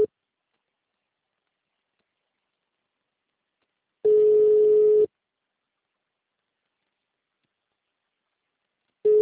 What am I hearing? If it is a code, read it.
Ringback tone (Europe, 425 Hz).